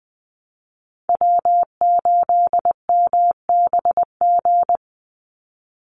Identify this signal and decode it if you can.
Morse code, heard as a keyed tone, about 20 wpm, 700 Hz. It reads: W8MBG
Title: USA Call Signs 20wpm